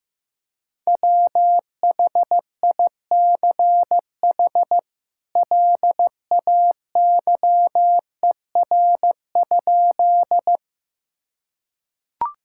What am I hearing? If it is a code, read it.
Morse code, heard as a keyed tone, about 15 wpm, 700 Hz. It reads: WHICH LAYER?
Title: Sentences from Top 1600 Words 15wpm